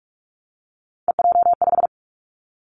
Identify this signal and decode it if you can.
Morse code, heard as a keyed tone, about 46 wpm, 700 Hz. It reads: EJ5
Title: Three Challenging Letter-Number Combinations 45wpm